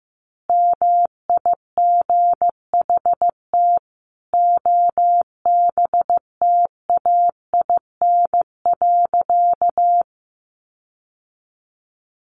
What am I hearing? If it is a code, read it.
Morse code, heard as a keyed tone, about 15 wpm, 700 Hz. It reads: MIGHT OBTAIN.
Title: Sentences from Top 1300 Words 15wpm